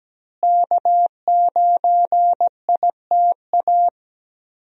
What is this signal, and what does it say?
Morse code, heard as a keyed tone, about 17 wpm, 700 Hz. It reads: K9ITA
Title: USA Call Signs - 5 Characters - Repeated 2x - 17wpm